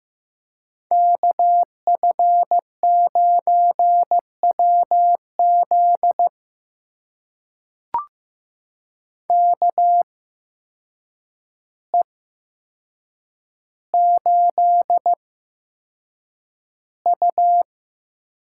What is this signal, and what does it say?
Morse code, heard as a keyed tone, about 15 wpm, spaced as for 7 wpm, 700 Hz. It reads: KF9WZ KE8U
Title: USA Call Signs - 5 Characters - 8x Character Spacing - 15wpm